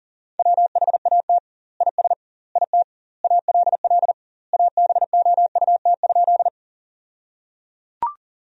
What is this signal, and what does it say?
Morse code, heard as a keyed tone, about 37 wpm, 700 Hz. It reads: WHAT IS IT ALL ABOUT?
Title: 2x Word-Spacing - Sentences from Top 100 Words - 40wpm